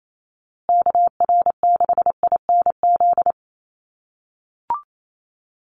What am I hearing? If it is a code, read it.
Morse code, heard as a keyed tone, about 28 wpm, 700 Hz. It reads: KR6INZ
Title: USA Call Signs Repeated 2x 28wpm